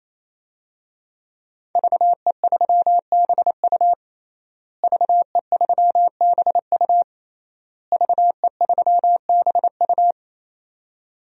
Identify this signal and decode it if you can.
Morse code, heard as a keyed tone, about 28 wpm, 700 Hz. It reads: VE3BU VE3BU VE3BU
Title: DXCC-100 Call Signs - 5 Characters - Repeated 3x - 28wpm